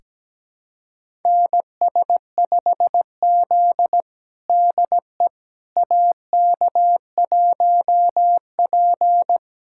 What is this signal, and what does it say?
Morse code, heard as a keyed tone, about 17 wpm, 700 Hz. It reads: NS5Z DE AK1P
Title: USA Callsign DE Callsign - 4 Characters - 17wpm